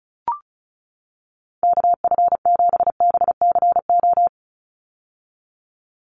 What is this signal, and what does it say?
Morse code, heard as a keyed tone, about 36 wpm, 700 Hz. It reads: KF7BCO
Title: USA Call Signs 35wpm